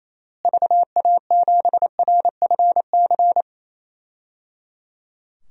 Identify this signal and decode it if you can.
Morse code, heard as a keyed tone, about 28 wpm, 700 Hz. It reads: VA7RFC
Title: DXCC-100 Call Signs - 6 Characters - Repeated 3x - 28wpm